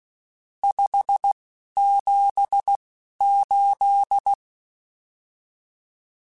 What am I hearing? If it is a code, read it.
Morse code, heard as a keyed tone, about 16 wpm, 800 Hz. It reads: 5 7 8
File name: RSTs / RST-mp3s / Various RST @10